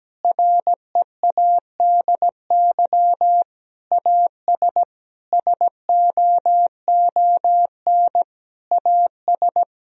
Morse code, heard as a keyed tone, about 17 wpm, 700 Hz. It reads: READY AS SOON AS
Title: Sentences from Top 1000 Words - Review Part 2 - 17wpm